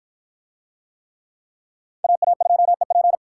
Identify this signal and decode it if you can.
Morse code, heard as a keyed tone, about 48 wpm, 700 Hz. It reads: AA2EP